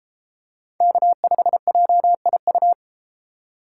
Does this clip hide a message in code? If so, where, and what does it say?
Yes: Morse code from 0.8 to 2.7 s, heard as a keyed tone, about 32 wpm, 700 Hz. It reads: K5JIU